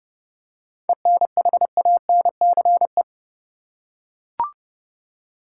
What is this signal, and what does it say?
Morse code, heard as a keyed tone, about 30 wpm, 700 Hz. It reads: ENHANCE
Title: Seven Letter Words - 30wpm